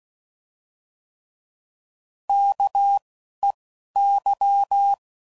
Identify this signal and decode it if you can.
Morse code, heard as a keyed tone, about 16 wpm, 800 Hz. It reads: K E Y